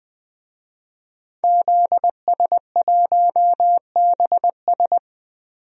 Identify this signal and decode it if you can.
Morse code, heard as a keyed tone, about 20 wpm, 700 Hz. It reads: ZS1BS
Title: DXCC-100 Call Signs - 5 Characters - 20wpm